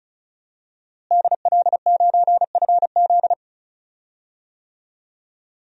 Morse code, heard as a keyed tone, about 36 wpm, 700 Hz. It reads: DL9FZ
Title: DXCC-100 Call Signs - 5 Characters - Repeated 3x - 35wpm